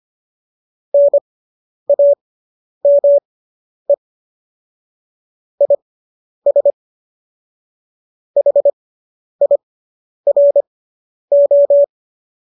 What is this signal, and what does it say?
Morse code, heard as a keyed tone, about 25 wpm, spaced as for 10 wpm, 570 Hz. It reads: NAME IS HIRO
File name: qso005 10